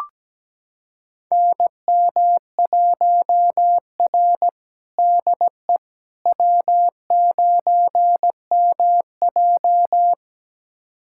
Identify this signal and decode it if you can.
Morse code, heard as a keyed tone, about 17 wpm, 700 Hz. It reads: NM1R DE W9MJ
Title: USA Callsign DE Callsign - 4 Characters - 17wpm